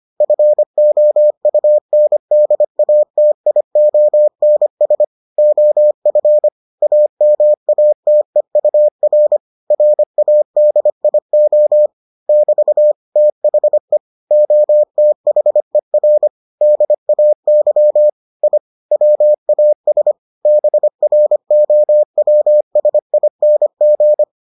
Morse code, heard as a keyed tone, about 25 wpm, 600 Hz. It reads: FOUNDATIONS OF AMATEUR RADIO <BT> THE OTHER DAY I WAS BROWSING